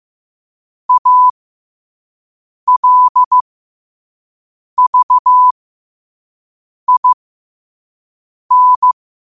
Morse code, heard as a keyed tone, about 15 wpm, spaced as for 5 wpm, 1000 Hz. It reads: ALVIN
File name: qotdfiles / 5wpm